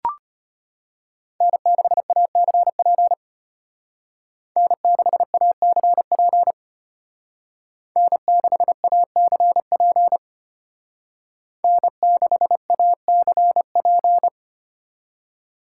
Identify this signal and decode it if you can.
Morse code, heard as a keyed tone, about 32 wpm, 700 Hz. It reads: N6ACP N6ACP N6ACP N6ACP